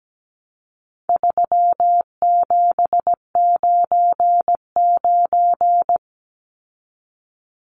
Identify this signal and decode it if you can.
Morse code, heard as a keyed tone, about 17 wpm, 700 Hz. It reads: 3799
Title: Numbers - 4-Digit 17wpm